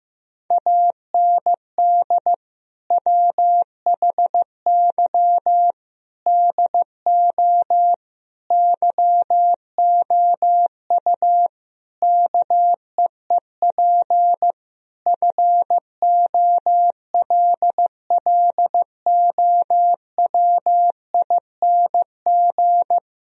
Morse code, heard as a keyed tone, about 16 wpm, 700 Hz. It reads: AND WHY DO YOU KEEP FOLLOWING
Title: Sentences from Top 1000 Words - Review Part 4 - 15wpm